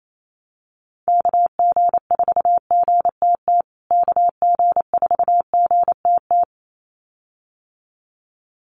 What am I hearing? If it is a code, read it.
Morse code, heard as a keyed tone, about 28 wpm, 700 Hz. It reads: KG4GTT KG4GTT
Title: USA Call Signs Repeated 2x 28wpm